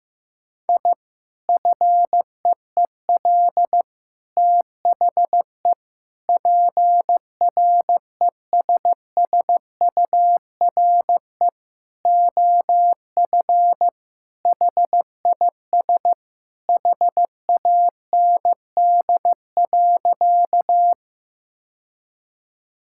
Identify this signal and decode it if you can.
Morse code, heard as a keyed tone, about 15 wpm, 700 Hz. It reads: I FEEL THE PRESSURE OF HIS HAND.
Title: Sentences from Top 1000 Words - Review Part 6 - 15wpm